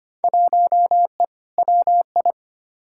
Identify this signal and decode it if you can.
Morse code, heard as a keyed tone, about 25 wpm, 700 Hz. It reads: 1E WI
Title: Field Day Exchanges - 25wpm